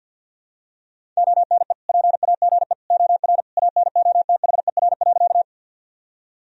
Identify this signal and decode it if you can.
Morse code, heard as a keyed tone, about 51 wpm, 700 Hz. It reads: ONE PAGE OR ANOTHER.
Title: Sentences from Top 1000 Words - Review Part 1 - 50wpm